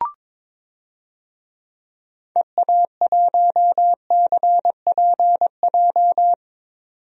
Morse code, heard as a keyed tone, about 22 wpm, 700 Hz. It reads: EA1CPJ
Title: DXCC-100 Call Signs - 6 Characters - 22wpm